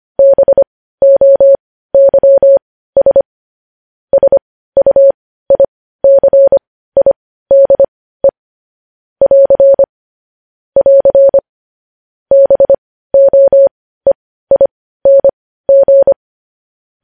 Morse code, heard as a keyed tone, about 25 wpm, spaced as for 15 wpm, 550 Hz. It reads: BOYS SUICIDE <AR> <AR> BOEING
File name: podcast / News headlines-1x-25x15